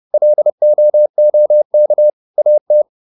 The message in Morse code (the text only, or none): LOOK AT